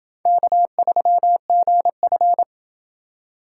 Morse code, heard as a keyed tone, about 28 wpm, 700 Hz. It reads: K3GF